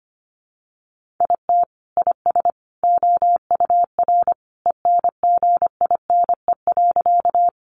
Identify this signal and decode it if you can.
Morse code, heard as a keyed tone, about 25 wpm, 700 Hz. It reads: IT IS OUR ENGINE.
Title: Sentences from Top 1500 Words 25wpm